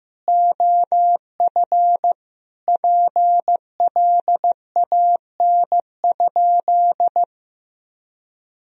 Morse code, heard as a keyed tone, about 15 wpm, 700 Hz. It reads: OF PLAN?
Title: Sentences from Top 1000 Words - Review Part 2 - 15wpm